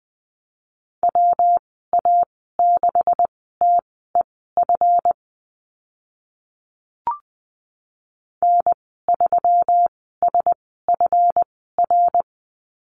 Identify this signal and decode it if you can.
Morse code, heard as a keyed tone, about 20 wpm, spaced as for 14 wpm, 700 Hz. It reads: WA6TEF N3SFR
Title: 147 Lesson 24g - Letter y - Call Signs 20wpm